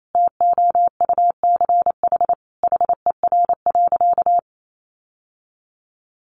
Morse code, heard as a keyed tone, about 28 wpm, 700 Hz. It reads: TOUCH HER.
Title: Sentences from Top 1200 Words 28wpm